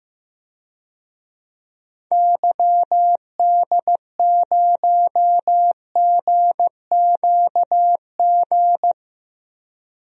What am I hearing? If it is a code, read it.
Morse code, heard as a keyed tone, about 15 wpm, 700 Hz. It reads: YD0GQG
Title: DXCC-100 Call Signs - 6 Characters - 15wpm